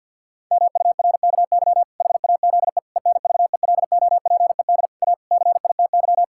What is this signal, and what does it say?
Morse code, heard as a keyed tone, about 50 wpm, 700 Hz. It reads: MURKY HAZE ENVELOPED A CITY
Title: Pangram Challenge v2 - 50wpm